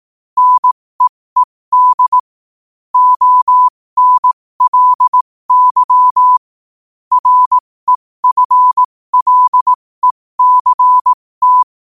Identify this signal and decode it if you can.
Morse code, heard as a keyed tone, about 18 wpm, spaced as for 15 wpm, 1000 Hz. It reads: NEED ONLY REFLECT
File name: qotdfiles / 15wpm